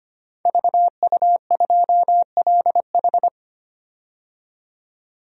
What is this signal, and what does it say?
Morse code, heard as a keyed tone, about 25 wpm, 700 Hz. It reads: VU2LH
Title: DXCC-100 Call Signs - 5 Characters - 25wpm